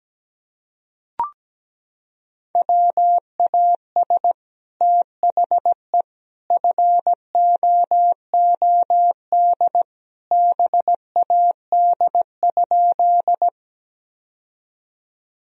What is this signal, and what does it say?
Morse code, heard as a keyed tone, about 17 wpm, 700 Hz. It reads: WAS THE FOOD BAD?
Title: Sentences from Top 1000 Words - Review Part 1 - 17wpm